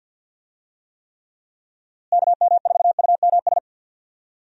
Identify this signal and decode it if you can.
Morse code, heard as a keyed tone, about 51 wpm, 700 Hz. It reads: KM4UMS
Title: USA Call Signs - 6 Characters - Repeated 2x - 50wpm